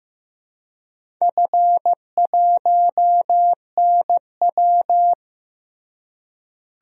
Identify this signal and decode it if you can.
Morse code, heard as a keyed tone, about 15 wpm, 700 Hz. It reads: F1NW
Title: DXCC-100 Call Signs - 4 Characters - Repeated 2x - 15wpm